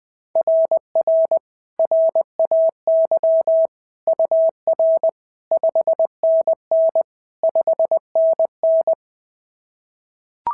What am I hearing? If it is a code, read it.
Morse code, heard as a keyed tone, about 20 wpm, 690 Hz. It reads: RR RAY UR 5NN 5NN
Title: SOTA QSO Practice 20wpm